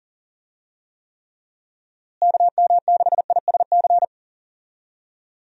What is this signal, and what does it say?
Morse code, heard as a keyed tone, about 40 wpm, 700 Hz. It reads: KM6ISC
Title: USA Call Signs - 6 Characters - Repeated 3x - 40wpm